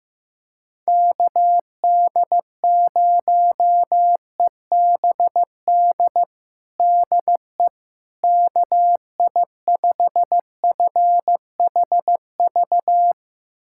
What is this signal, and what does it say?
Morse code, heard as a keyed tone, about 15 wpm, 700 Hz. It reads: KD0EBD DE KI5FHV
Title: USA Callsign DE Callsign - 6 Characters - 15wpm